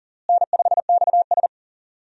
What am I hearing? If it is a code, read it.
Morse code, heard as a keyed tone, about 39 wpm, 700 Hz. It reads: N5XS
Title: USA Call Signs - 4 Characters - Repeated 2x - 40wpm